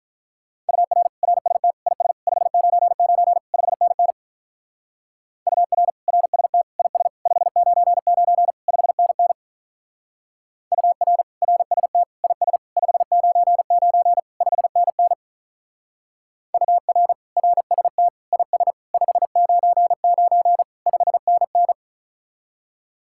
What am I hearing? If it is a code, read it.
Morse code, heard as a keyed tone, about 46 wpm, 700 Hz. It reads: UR RST IS 599 5NN UR RST IS 599 5NN UR RST IS 599 5NN UR RST IS 599 5NN